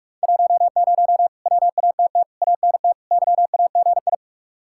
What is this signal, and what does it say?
Morse code, heard as a keyed tone, about 58 wpm, 700 Hz. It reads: 10 WATT ANT YAGI